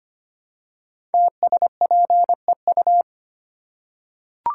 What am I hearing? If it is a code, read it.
Morse code, heard as a keyed tone, about 25 wpm, 700 Hz. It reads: TSPEU
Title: Code Groups - 5 Letter:Numbers - 25wpm